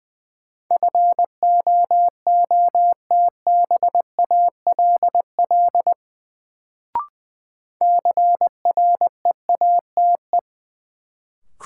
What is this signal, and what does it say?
Morse code, heard as a keyed tone, about 20 wpm, 700 Hz. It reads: FOOTBALL CREATE
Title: Sentences from Top 1600 Words - New Word Drill Alt - 20wpm